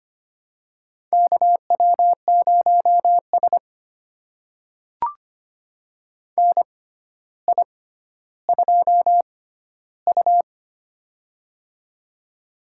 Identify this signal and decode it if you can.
Morse code, heard as a keyed tone, about 25 wpm, spaced as for 13 wpm, 700 Hz. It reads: KW0S NI2U
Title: USA Call Signs - 4 Characters - 6x Character Spacing - 25wpm